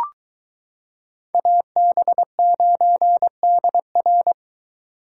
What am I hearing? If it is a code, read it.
Morse code, heard as a keyed tone, about 22 wpm, 700 Hz. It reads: AB9DR